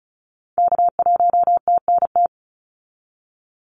Morse code, heard as a keyed tone, about 34 wpm, 700 Hz. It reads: K1TNT
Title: NA SOTA Activator Callsigns 35wpm